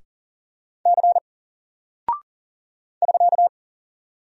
Morse code, heard as a keyed tone, about 40 wpm, 700 Hz. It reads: C <SK>
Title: Single Character (Letter:Number:Punctuation:Prosign) Alt 40wpm